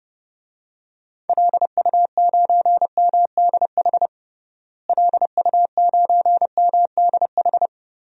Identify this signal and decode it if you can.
Morse code, heard as a keyed tone, about 32 wpm, 700 Hz. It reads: LU9MDH LU9MDH
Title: DXCC-100 Call Signs - 6 Characters - Repeated 2x - 30wpm